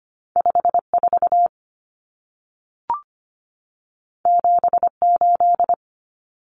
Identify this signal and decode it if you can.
Morse code, heard as a keyed tone, about 25 wpm, 700 Hz. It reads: 54 78
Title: Numbers - 2-Digit 25wpm